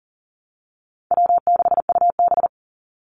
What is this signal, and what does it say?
Morse code, heard as a keyed tone, about 40 wpm, 700 Hz. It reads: W6UB
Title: NA SOTA Chaser Callsigns 40wpm